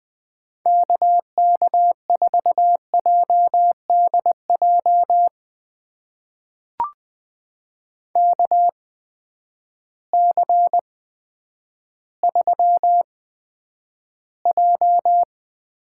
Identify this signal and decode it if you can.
Morse code, heard as a keyed tone, about 20 wpm, spaced as for 10 wpm, 700 Hz. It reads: KK4JDJ KC3J
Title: USA Call Signs - 6 Characters - 8x Character Spacing - 20wpm